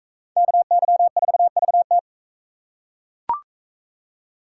Morse code, heard as a keyed tone, about 38 wpm, 700 Hz. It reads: KY4VT